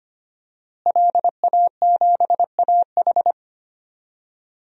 Morse code, heard as a keyed tone, about 25 wpm, 700 Hz. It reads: LA7AH
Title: DXCC-100 Call Signs - 5 Characters - Repeated 2x - 25wpm